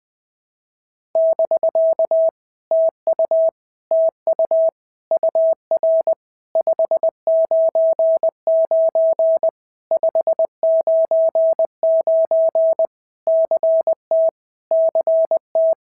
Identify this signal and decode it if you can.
Morse code, heard as a keyed tone, about 20 wpm, 650 Hz. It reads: <BK> TU TU UR 599 599 CT CT
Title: POTA QSO Practice - 20wpm